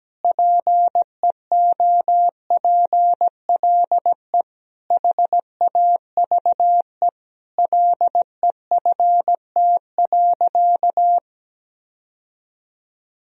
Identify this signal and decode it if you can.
Morse code, heard as a keyed tone, about 17 wpm, 700 Hz. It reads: PEOPLE HAVE LEFT.
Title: Sentences from Top 1000 Words - Review Part 1 - 17wpm